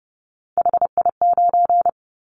Morse code, heard as a keyed tone, about 31 wpm, 700 Hz. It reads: HI9
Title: Callsign Prefixes for Top 100 DXCC - 30wpm